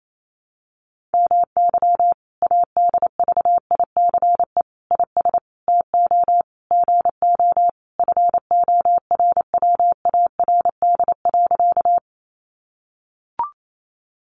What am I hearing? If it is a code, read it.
Morse code, heard as a keyed tone, about 28 wpm, 700 Hz. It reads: MY ADVICE IS TO GO FORWARD.
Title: Sentences from Top 1300 Words 28wpm